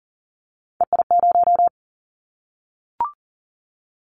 Morse code, heard as a keyed tone, about 42 wpm, 700 Hz. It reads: EI0
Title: Callsign Prefixes for Top 50 DXCC - 40wpm